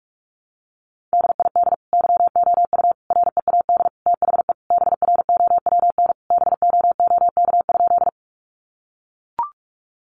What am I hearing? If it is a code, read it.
Morse code, heard as a keyed tone, about 44 wpm, 700 Hz. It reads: DID YOU READ THE BROWN BOOK?
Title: Sentences from Top 1200 Words 45wpm